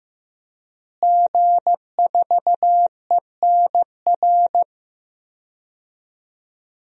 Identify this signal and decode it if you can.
Morse code, heard as a keyed tone, about 15 wpm, 700 Hz. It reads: G4ENR
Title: DXCC-100 Call Signs - 5 Characters - Repeated 3x - 15wpm